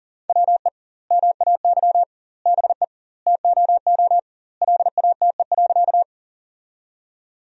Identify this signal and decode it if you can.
Morse code, heard as a keyed tone, about 40 wpm, 700 Hz. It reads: WE MAY BE TOO LATE.
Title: 2x Word-Spacing - Sentences from Top 300 Words - 40wpm